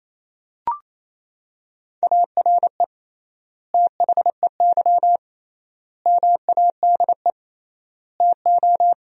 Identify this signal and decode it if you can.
Morse code, heard as a keyed tone, about 27 wpm, 700 Hz. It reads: ARE THEY MADE TO
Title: 3x Word-Spacing - Sentences from Top 400 Words - 28wpm